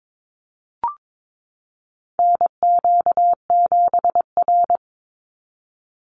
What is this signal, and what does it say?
Morse code, heard as a keyed tone, about 22 wpm, 700 Hz. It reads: NQ7R
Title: NA SOTA Chaser Callsigns 22wpm